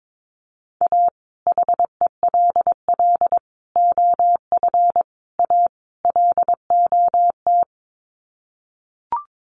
Morse code, heard as a keyed tone, about 22 wpm, 700 Hz. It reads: A HELL OF A LOT
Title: Sets of 5 Words Encore 22wpm